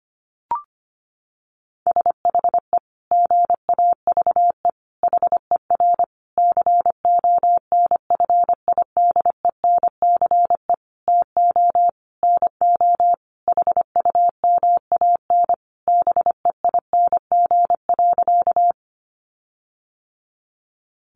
Morse code, heard as a keyed tone, about 25 wpm, 700 Hz. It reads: SHE GAVE HER CONFIDENCE TO NO HUMAN BEING.
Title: Sentences from Top 1500 Words 25wpm